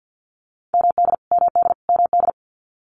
Morse code, heard as a keyed tone, about 50 wpm, 700 Hz. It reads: MD MD MD
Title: US States Repeated 3x 50wpm